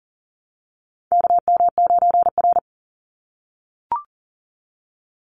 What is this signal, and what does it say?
Morse code, heard as a keyed tone, about 40 wpm, 700 Hz. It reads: KM9R
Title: Call Signs - 40wpm